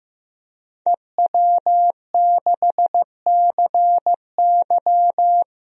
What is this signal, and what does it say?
Morse code, heard as a keyed tone, about 16 wpm, 700 Hz. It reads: EW6CY
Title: DXCC-100 Call Signs - 5 Characters - 15wpm